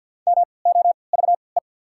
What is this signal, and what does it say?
Morse code, heard as a keyed tone, about 49 wpm, spaced as for 28 wpm, 690 Hz. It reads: MOVE
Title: 3x Character Spacing - Top 200 Words - 50wpm